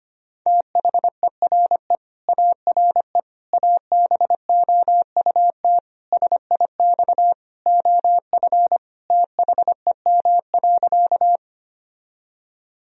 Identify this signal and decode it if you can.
Morse code, heard as a keyed tone, about 25 wpm, 700 Hz. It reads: THERE ARE ABOUT SIX OF THEM.
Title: Sentences from Top 1000 Words - Review Part 2 - 25wpm